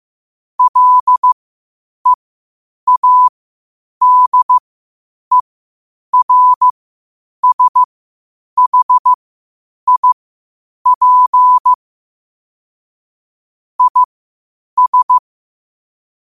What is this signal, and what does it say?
Morse code, heard as a keyed tone, about 15 wpm, spaced as for 8 wpm, 1000 Hz. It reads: LEADERSHIP IS